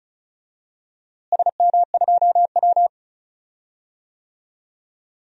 Morse code, heard as a keyed tone, about 36 wpm, 700 Hz. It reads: SM2W
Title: DXCC-100 Call Signs - 4 Characters - Repeated 3x - 35wpm